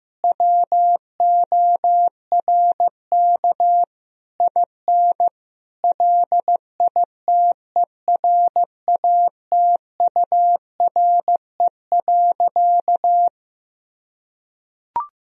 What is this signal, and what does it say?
Morse code, heard as a keyed tone, about 15 wpm, 700 Hz. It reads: WORK IN LITERATURE.
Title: Sentences from Top 1000 Words - Review Part 3 - 15wpm